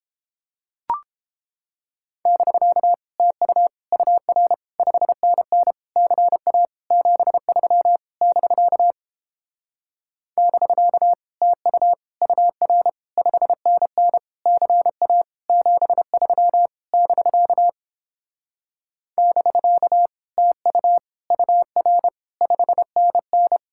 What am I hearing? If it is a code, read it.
Morse code, heard as a keyed tone, about 29 wpm, 700 Hz. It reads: <BK> TU UR 5NN CA 73 <BK> <BK> TU UR 5NN CA 73 <BK> <BK> TU UR 5NN